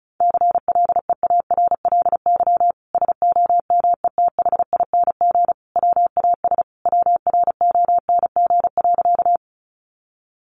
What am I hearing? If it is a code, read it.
Morse code, heard as a keyed tone, about 34 wpm, 700 Hz. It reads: CLEARLY SOMETHING WAS WRONG.
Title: Sentences from Top 1300 Words 35wpm